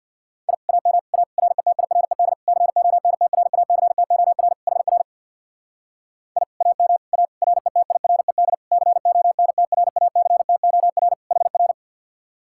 Morse code, heard as a keyed tone, about 63 wpm, 700 Hz. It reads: I AM A RETIRED CONTRACTOR HR I AM A RETIRED CONTRACTOR HR